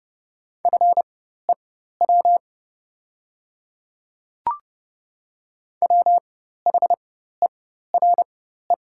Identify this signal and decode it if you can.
Morse code, heard as a keyed tone, about 30 wpm, spaced as for 14 wpm, 700 Hz. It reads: FEW WHERE
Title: 4x Character Spacing - Top 300 Words - 30wpm